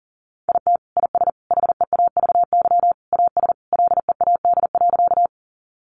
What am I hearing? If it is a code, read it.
Morse code, heard as a keyed tone, about 40 wpm, 700 Hz. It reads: IT IS HEAVY AS LEAD.
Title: Sentences from Top 1100 Words 40wpm